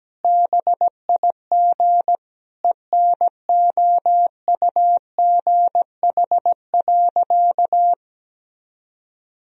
Morse code, heard as a keyed tone, about 17 wpm, 700 Hz. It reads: BIG ENOUGH.
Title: Sentences from Top 1000 Words - Review Part 1 - 17wpm